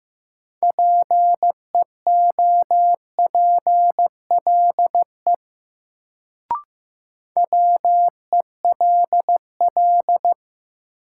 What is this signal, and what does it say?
Morse code, heard as a keyed tone, about 15 wpm, 700 Hz. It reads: PEOPLE WELL
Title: Sets of 2 Words - New Word Drill - 15wpm